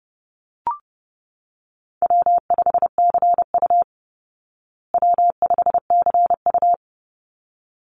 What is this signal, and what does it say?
Morse code, heard as a keyed tone, about 30 wpm, 700 Hz. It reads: W5CU W5CU
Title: Call Signs Repeated 3x - 30wpm